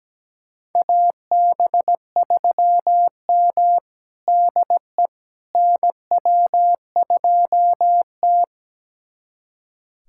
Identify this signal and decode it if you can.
Morse code, heard as a keyed tone, about 17 wpm, 700 Hz. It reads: AB3M DE NW2T
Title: USA Callsign DE Callsign - 4 Characters - 17wpm